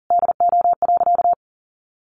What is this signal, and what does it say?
Morse code, heard as a keyed tone, about 40 wpm, 700 Hz. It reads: DO.
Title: Sentences from Top 600 Words 40wpm